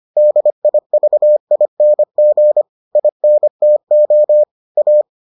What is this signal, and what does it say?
Morse code, heard as a keyed tone, about 25 wpm, 600 Hz. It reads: DIVING INTO A